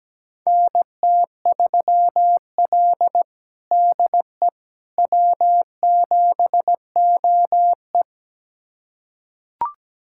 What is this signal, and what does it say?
Morse code, heard as a keyed tone, about 17 wpm, 700 Hz. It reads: NT3L DE W7OE
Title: USA Callsign DE Callsign - 4 Characters - 17wpm